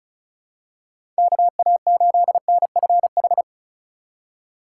Morse code, heard as a keyed tone, about 34 wpm, 700 Hz. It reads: KA8NFH
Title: USA Call Signs - 6 Characters - 2x Character Spacing - 35wpm